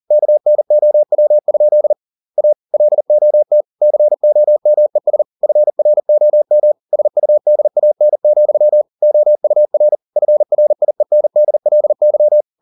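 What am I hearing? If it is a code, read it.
Morse code, heard as a keyed tone, about 40 wpm, 600 Hz. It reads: KNOW? A LOT COMES FROM SUDAN, OUR FRIENDLY
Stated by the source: Morse code at 40 wpm, practice text, CW med Gnister.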